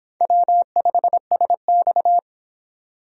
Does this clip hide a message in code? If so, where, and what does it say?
Yes: Morse code from 0.2 to 2.2 s, heard as a keyed tone, about 25 wpm, 700 Hz. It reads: W5SX